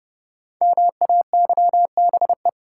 Morse code, heard as a keyed tone, about 30 wpm, 700 Hz. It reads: MAYBE
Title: Sentences from Top 1000 Words - New Word Drill - 30wpm